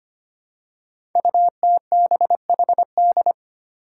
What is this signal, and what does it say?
Morse code, heard as a keyed tone, about 25 wpm, 700 Hz. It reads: UTBHD
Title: Five Challenging Letter-Number Combinations 25wpm